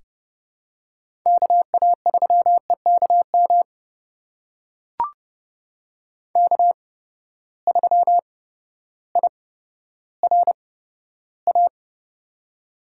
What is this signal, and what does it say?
Morse code, heard as a keyed tone, about 30 wpm, spaced as for 14 wpm, 700 Hz. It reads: KA3EKM K3IRA
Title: USA Call Signs - 8x Character Spacing - 30wpm